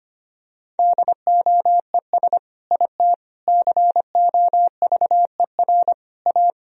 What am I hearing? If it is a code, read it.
Morse code, heard as a keyed tone, about 25 wpm, 700 Hz. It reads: DOES IT COVER A
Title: Sentences from Top 1000 Words - Review Part 2 - 25wpm